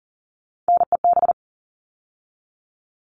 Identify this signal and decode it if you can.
Morse code, heard as a keyed tone, about 40 wpm, 700 Hz. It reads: NEB
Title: All Cut Numbers - 3-Digit 40wpm